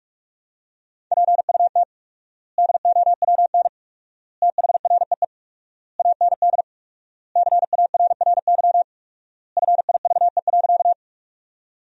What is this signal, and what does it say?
Morse code, heard as a keyed tone, about 46 wpm, 700 Hz. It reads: PUT DOWN THREE AND CARRY FIVE.
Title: 4x Word-Spacing - Sentences from Top 400 Words - 45wpm